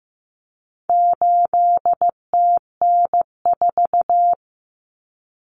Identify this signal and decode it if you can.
Morse code, heard as a keyed tone, about 15 wpm, 700 Hz. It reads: 8TN4
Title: Common Cut Numbers - 4-Digit 15wpm